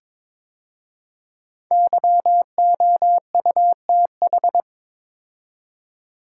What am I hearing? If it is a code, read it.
Morse code, heard as a keyed tone, about 22 wpm, 700 Hz. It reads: YOUTH
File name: 5 Letter Words - First 3 Letters Given - 22wpm